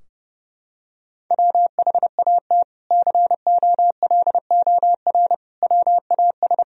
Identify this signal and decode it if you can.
Morse code, heard as a keyed tone, about 30 wpm, 700 Hz. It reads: WHAT COLOR WAS
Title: Sentences from Top 1000 Words - Review Part 2 - 30wpm